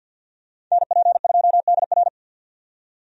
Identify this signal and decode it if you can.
Morse code, heard as a keyed tone, about 52 wpm, 700 Hz. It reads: NP2DR
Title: USA Call Signs - 5 Characters - 50wpm